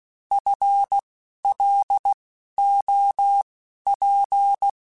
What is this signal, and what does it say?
Morse code, heard as a keyed tone, about 16 wpm, spaced as for 11 wpm, 800 Hz. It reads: FLOP
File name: Spaced / FL Runs sp mp3s / FL Runs sp @15